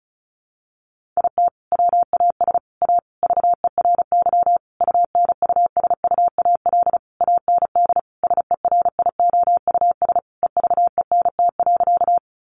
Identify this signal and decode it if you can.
Morse code, heard as a keyed tone, about 34 wpm, 700 Hz. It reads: IT WAS A VERY UNUSUAL AND SERIOUS EVENT.
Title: Sentences from Top 1400 Words 35wpm